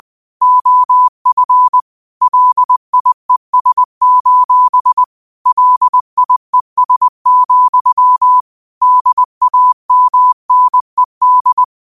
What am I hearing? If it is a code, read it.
Morse code, heard as a keyed tone, about 20 wpm, 1000 Hz. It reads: OF LIES: LIES, DAMNED